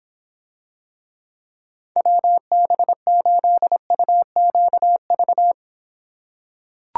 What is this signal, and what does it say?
Morse code, heard as a keyed tone, about 26 wpm, 700 Hz. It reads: WB8UQV